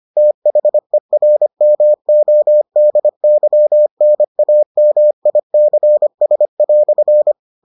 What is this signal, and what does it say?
Morse code, heard as a keyed tone, about 25 wpm, 600 Hz. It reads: THERMODYNAMICS"